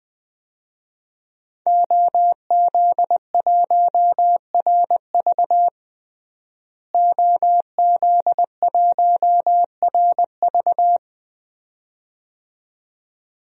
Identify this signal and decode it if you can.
Morse code, heard as a keyed tone, about 20 wpm, 700 Hz. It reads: OZ1RV OZ1RV
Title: DXCC-100 Call Signs - 5 Characters - Repeated 2x - 20wpm